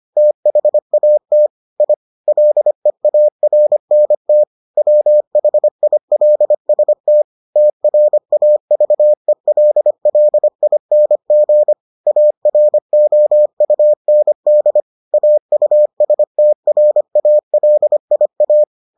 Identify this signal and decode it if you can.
Morse code, heard as a keyed tone, about 25 wpm, 600 Hz. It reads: THAT I LEARNT WHILST TRAVELLING AROUND AUSTRALIA